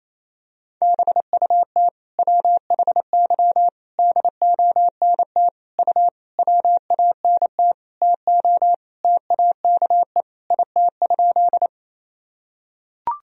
Morse code, heard as a keyed tone, about 28 wpm, 700 Hz. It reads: BUT WHY DONT U WANT TO TAKE IT?
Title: Sentences from Top 300 Words and Abbreviations 28wpm